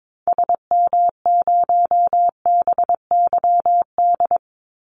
Morse code, heard as a keyed tone, about 22 wpm, 700 Hz. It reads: SM0BYD
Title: Call Signs Repeated 2x - 22wpm